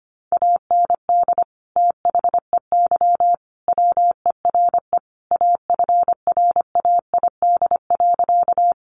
Morse code, heard as a keyed tone, about 25 wpm, 700 Hz. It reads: AND THEY WERE AFRAID.
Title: Sentences from Top 1500 Words 25wpm